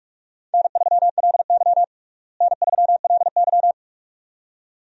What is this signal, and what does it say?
Morse code, heard as a keyed tone, about 46 wpm, 700 Hz. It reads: N3LY N3LY
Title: USA Call Signs - 4 Characters - Repeated 2x - 45wpm